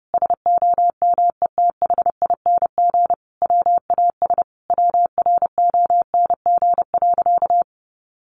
Morse code, heard as a keyed tone, about 30 wpm, 700 Hz. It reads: SOMETHING WAS WRONG.
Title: Sentences from Top 1300 Words 30wpm